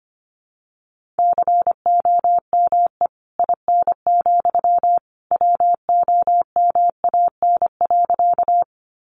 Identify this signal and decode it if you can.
Morse code, heard as a keyed tone, about 25 wpm, 700 Hz. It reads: COME IN, WOMAN.
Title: Sentences from Top 1100 Words 25wpm